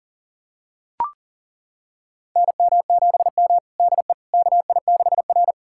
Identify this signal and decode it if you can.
Morse code, heard as a keyed tone, about 40 wpm, 700 Hz. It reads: NM7M DE KI6R
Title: USA Callsign DE Callsign - 4 Characters - 40wpm